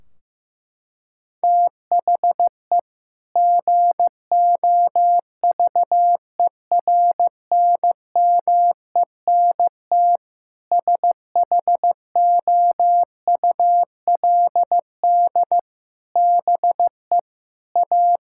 Morse code, heard as a keyed tone, about 16 wpm, 700 Hz. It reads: THE GOVERNMENT SHOULD BE A
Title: Sentences from Top 1600 Words 15wpm